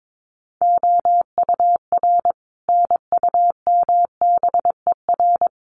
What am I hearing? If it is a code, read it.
Morse code, heard as a keyed tone, about 22 wpm, 700 Hz. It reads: OUR NUMBER
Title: Sets of 5 Words Encore2 22wpm